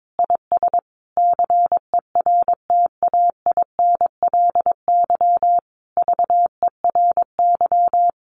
Morse code, heard as a keyed tone, about 22 wpm, 700 Hz. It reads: IS CERTAINLY VERY
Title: Sentences from Top 1300 Words 22wpm